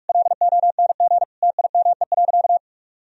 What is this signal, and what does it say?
Morse code, heard as a keyed tone, about 41 wpm, 700 Hz. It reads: LONG TIME.
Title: Sentences from Top 1000 Words - Review Part 2 - 45wpm